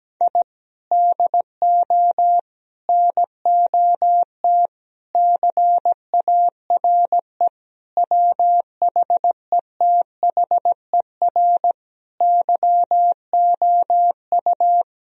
Morse code, heard as a keyed tone, about 17 wpm, 700 Hz. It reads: I DO NOT CARE WHETHER YOU
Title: Sentences from Top 1000 Words - Review Part 3 - 17wpm